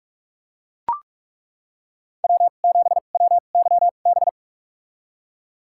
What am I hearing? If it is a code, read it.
Morse code, heard as a keyed tone, about 41 wpm, spaced as for 31 wpm, 700 Hz. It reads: W7WYB
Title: USA Call Signs - 5 Characters - 2x Character Spacing - 45wpm